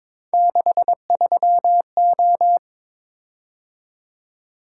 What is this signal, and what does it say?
Morse code, heard as a keyed tone, about 22 wpm, 700 Hz. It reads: 63O
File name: practice / 22wpm / Three Letter-Number Combinations - 22wpm